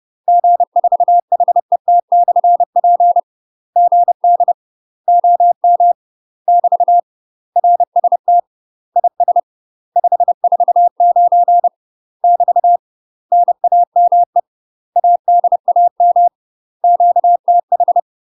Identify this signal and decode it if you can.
Morse code, heard as a keyed tone, about 30 wpm, 700 Hz. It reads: G4HET/P GD OM <BT> RST IS 549 <BT> NAME ADAM QTH
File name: UK MASTER QSOs TRAINING/30WPM